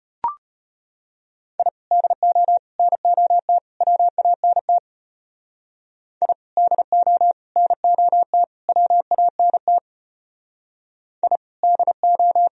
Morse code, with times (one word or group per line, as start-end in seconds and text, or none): 1.6-1.7 I
1.9-2.6 DO
2.8-3.6 NOT
3.8-4.8 WANT
6.2-6.3 I
6.6-7.3 DO
7.6-8.4 NOT
8.7-9.8 WANT
11.2-11.4 I
11.6-12.5 DO